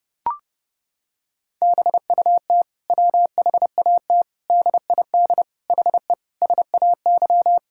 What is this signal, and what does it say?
Morse code, heard as a keyed tone, about 30 wpm, 700 Hz. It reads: BUT WHAT DID HE SAY
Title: Sentences from Top 1000 Words - Review Part 6 - 30wpm